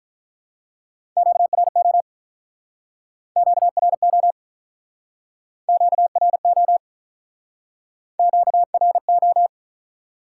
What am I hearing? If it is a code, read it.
Morse code, heard as a keyed tone, about 48 wpm, 700 Hz. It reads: QRO QRO QRO QRO